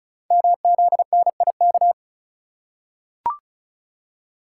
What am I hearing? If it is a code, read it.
Morse code, heard as a keyed tone, about 36 wpm, 690 Hz. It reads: MZNIK
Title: Code Groups - 5 Letters - 35wpm